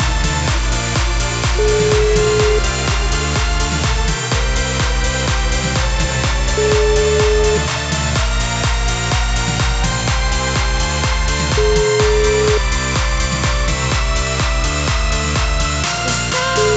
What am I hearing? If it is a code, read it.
Ringback tone (Europe, 425 Hz), with noise.